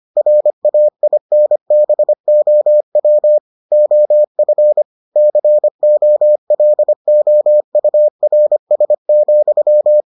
Morse code, heard as a keyed tone, about 25 wpm, 600 Hz. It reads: RAINBOW OF COLOURS,